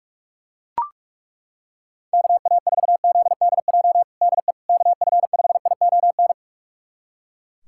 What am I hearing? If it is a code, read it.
Morse code, heard as a keyed tone, about 42 wpm, 700 Hz. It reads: KA4ZDJ DE KF5ION
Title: USA Callsign DE Callsign - 6 Characters - 45wpm